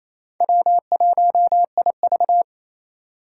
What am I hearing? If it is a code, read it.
Morse code, heard as a keyed tone, about 28 wpm, 700 Hz. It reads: W1IV
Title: Four Challenging Letter-Number Combinations 28wpm